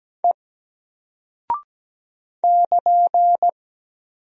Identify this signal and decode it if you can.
Morse code, heard as a keyed tone, about 17 wpm, 700 Hz. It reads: E <KN>
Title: Single Character (Letter:Number:Punctuation:Prosign) 17wpm